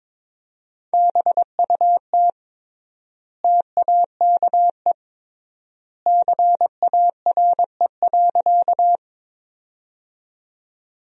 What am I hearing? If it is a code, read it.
Morse code, heard as a keyed tone, about 22 wpm, 700 Hz. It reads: BUT TAKE CARE.
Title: 3x Word-Spacing - Sentences from Top 300 Words - 22wpm